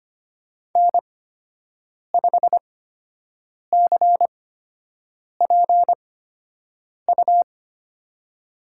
Morse code, heard as a keyed tone, about 25 wpm, spaced as for 7 wpm, 700 Hz. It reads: N5CPU
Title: USA Call Signs - 8x Character Spacing - 25wpm